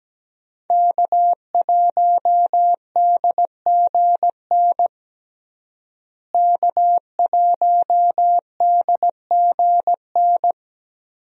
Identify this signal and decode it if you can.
Morse code, heard as a keyed tone, about 17 wpm, 700 Hz. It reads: K1DGN K1DGN
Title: USA Call Signs - 5 Characters - Repeated 2x - 17wpm